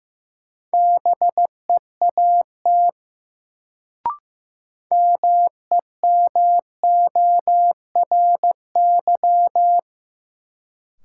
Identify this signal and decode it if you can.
Morse code, heard as a keyed tone, about 16 wpm, 700 Hz. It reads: BEAT MEMORY
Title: Sentences from Top 1300 Words - New Word Drill - 15wpm